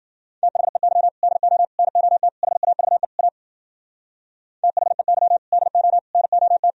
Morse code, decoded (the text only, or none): THEY DO NOT HAVE A THEY DO NOT